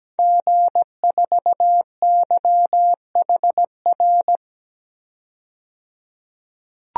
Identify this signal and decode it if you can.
Morse code, heard as a keyed tone, about 17 wpm, 700 Hz. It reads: G4YHR
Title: DXCC-100 Call Signs - 5 Characters - 17wpm